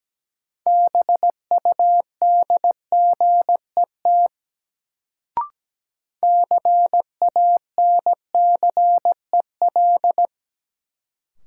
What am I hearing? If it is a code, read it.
Morse code, heard as a keyed tone, about 17 wpm, 700 Hz. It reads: BUDGET CANCEL
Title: Sentences from Top 1200 Words - New Word Drill - 17wpm